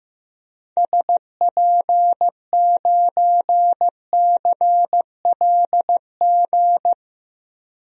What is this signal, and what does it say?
Morse code, heard as a keyed tone, about 15 wpm, 700 Hz. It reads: SP9CLG
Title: DXCC-100 Call Signs - 6 Characters - 15wpm